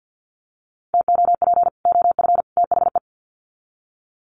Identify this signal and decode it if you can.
Morse code, heard as a keyed tone, about 50 wpm, 700 Hz. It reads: TOP OF THE
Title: Sets of 3 Words 50wpm